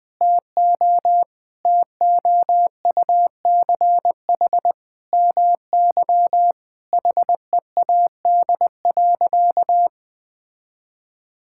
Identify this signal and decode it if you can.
Morse code, heard as a keyed tone, about 21 wpm, 700 Hz. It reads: TO TOUCH MY HEAD.
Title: Sentences from Top 1600 Words 20wpm